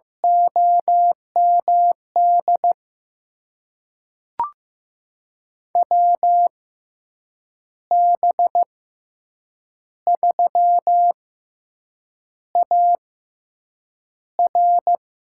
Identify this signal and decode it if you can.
Morse code, heard as a keyed tone, about 15 wpm, spaced as for 7 wpm, 700 Hz. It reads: OMD WB3AR
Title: USA Call Signs - 6 Characters - 6x Character Spacing - 15wpm